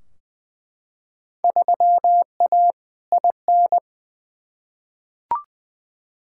Morse code, heard as a keyed tone, about 21 wpm, 700 Hz. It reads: 3A IN
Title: Field Day Exchanges - 20wpm